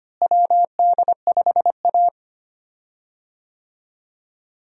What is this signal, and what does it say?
Morse code, heard as a keyed tone, about 25 wpm, 700 Hz. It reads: WD5A
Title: USA Call Signs - 4 Characters - Repeated 3x - 25wpm